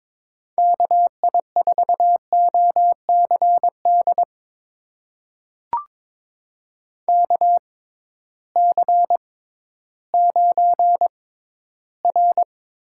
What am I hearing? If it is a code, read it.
Morse code, heard as a keyed tone, about 22 wpm, spaced as for 13 wpm, 700 Hz. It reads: KI4OCD KC9R
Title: USA Call Signs - 6 Characters - 6x Character Spacing - 22wpm